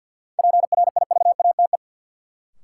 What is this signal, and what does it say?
Morse code, heard as a keyed tone, about 52 wpm, 700 Hz. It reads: PRIVATE
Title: Sentences from Top 1000 Words - New Word Drill - 50wpm